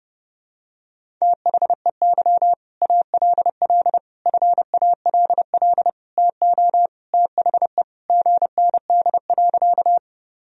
Morse code, heard as a keyed tone, about 32 wpm, 700 Hz. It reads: THEY ALL FALL TO THE GND.
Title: Sentences from Top 500 Words and Abbreviations 30wpm